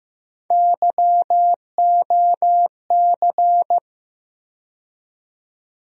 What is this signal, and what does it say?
Morse code, heard as a keyed tone, about 15 wpm, 700 Hz. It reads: YOC
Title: USA Call Sign Postfixes - 3 Letters 15wpm